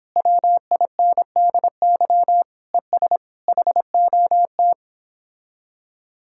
Morse code, heard as a keyed tone, about 29 wpm, 700 Hz. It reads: WINDY ES HOT